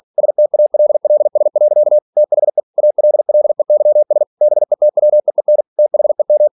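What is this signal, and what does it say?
Morse code, heard as a keyed tone, about 46 wpm, 600 Hz. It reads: STALLS. THE ALLEYS BETWEEN THEM
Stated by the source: Practice text from an English novel sent at 47 WPM.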